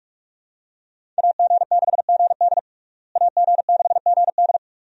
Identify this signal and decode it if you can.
Morse code, heard as a keyed tone, about 46 wpm, 700 Hz. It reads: AG6GD AG6GD
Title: USA Call Signs - 5 Characters - Repeated 2x - 45wpm